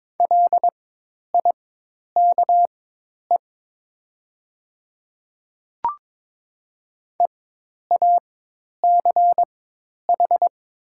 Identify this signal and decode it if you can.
Morse code, heard as a keyed tone, about 22 wpm, spaced as for 10 wpm, 700 Hz. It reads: LIKE EACH
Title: 4x Character Spacing - Top 200 Words - 22wpm